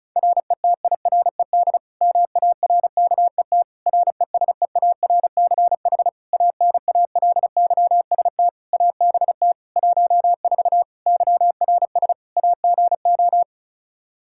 Morse code, heard as a keyed tone, about 36 wpm, 700 Hz. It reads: RETIRED MARKET RESEARCH ANALYST ABT 14 YRS AGO
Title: Practice QSO - Exchange 3 - More Info - 35wpm